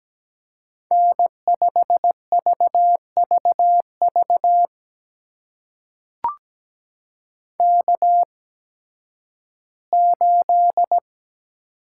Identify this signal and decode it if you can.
Morse code, heard as a keyed tone, about 17 wpm, spaced as for 11 wpm, 700 Hz. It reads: N5VVV K8
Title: USA Call Signs - 5 Characters - 8x Character Spacing - 17wpm